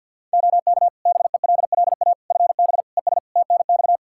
Morse code, heard as a keyed tone, about 52 wpm, 700 Hz. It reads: OK BELLA FB ES TNX
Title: K1USN SST QSO Practice 50wpm